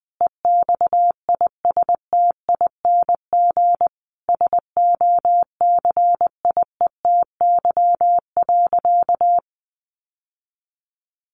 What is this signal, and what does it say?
Morse code, heard as a keyed tone, about 20 wpm, 700 Hz. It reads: EXISTING SOCIETY.
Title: Sentences from Top 1500 Words 20wpm